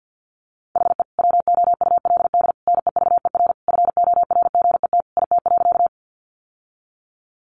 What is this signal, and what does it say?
Morse code, heard as a keyed tone, about 50 wpm, 700 Hz. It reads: HE WOULD NEVER FORGET IT.
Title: Sentences from Top 1200 Words 50wpm